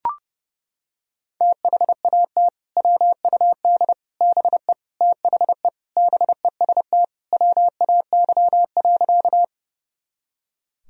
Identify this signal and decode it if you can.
Morse code, heard as a keyed tone, about 32 wpm, 700 Hz. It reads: THAT WUD BE THE BEST WAY.
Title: Sentences from Top 400 Words and Abbreviations 30wpm